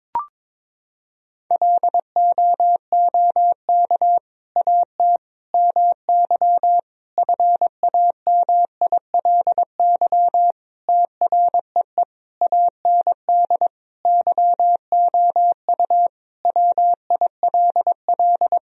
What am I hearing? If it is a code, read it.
Morse code, heard as a keyed tone, about 22 wpm, 700 Hz. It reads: LOOK AT MY FAMILY TREE AND YOU WILL
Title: Sentences from Top 1000 Words - Review Part 1 - 22wpm